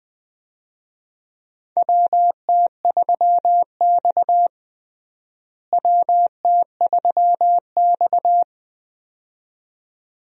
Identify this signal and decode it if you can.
Morse code, heard as a keyed tone, about 20 wpm, 700 Hz. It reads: WT3X WT3X
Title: USA Call Signs - 4 Characters - Repeated 2x - 20wpm